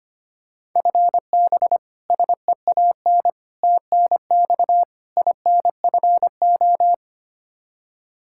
Morse code, heard as a keyed tone, about 25 wpm, 700 Hz. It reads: FB SEAN TNX INFO
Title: Practice QSO - Exchange 3 - More Info - 25wpm